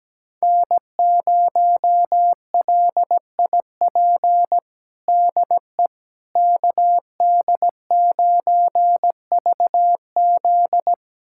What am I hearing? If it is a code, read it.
Morse code, heard as a keyed tone, about 17 wpm, 700 Hz. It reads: N0LIP DE KD9VZ
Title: USA Callsign DE Callsign - 5 Characters - 17wpm